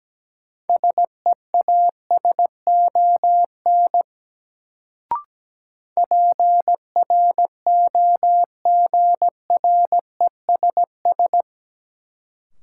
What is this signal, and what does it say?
Morse code, heard as a keyed tone, about 17 wpm, 700 Hz. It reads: SEASON PROGRESS
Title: Sentences from Top 1400 Words - New Word Drill - 17wpm